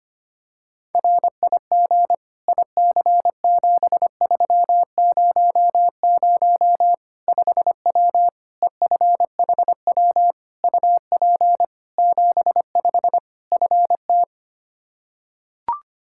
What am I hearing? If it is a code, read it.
Morse code, heard as a keyed tone, about 25 wpm, 700 Hz. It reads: RIG IC7300 5W EFHW UP 75 FT
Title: Practice QSO - Exchange 2 - Info - 25wpm